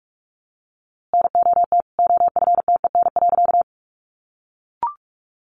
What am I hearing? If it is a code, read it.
Morse code, heard as a keyed tone, about 46 wpm, 700 Hz. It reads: NOT OFTEN.
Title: A Sound Of Thunder 45wpm